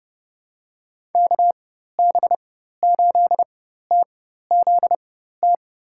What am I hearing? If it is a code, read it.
Morse code, heard as a keyed tone, about 32 wpm, spaced as for 14 wpm, 700 Hz. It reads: KB8TZT
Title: USA Call Signs - 6 Characters - 4x Character Spacing - 30wpm